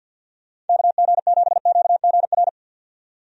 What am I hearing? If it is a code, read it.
Morse code, heard as a keyed tone, about 48 wpm, 700 Hz. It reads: KG7QGR
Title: USA Call Signs - 6 Characters - 50wpm